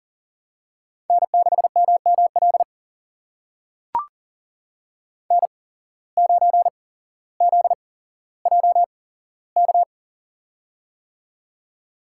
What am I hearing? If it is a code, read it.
Morse code, heard as a keyed tone, about 40 wpm, spaced as for 17 wpm, 700 Hz. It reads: N6MML N9ZJK
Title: USA Call Signs - 5 Characters - 8x Character Spacing - 40wpm